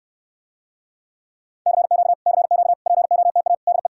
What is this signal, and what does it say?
Morse code, heard as a keyed tone, about 64 wpm, 700 Hz. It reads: CQ CQ POTA DE